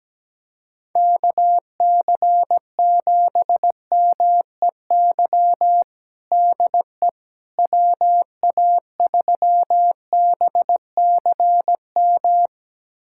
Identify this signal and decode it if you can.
Morse code, heard as a keyed tone, about 17 wpm, 700 Hz. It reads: KC7MEY DE WA3BCM
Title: USA Callsign DE Callsign 17wpm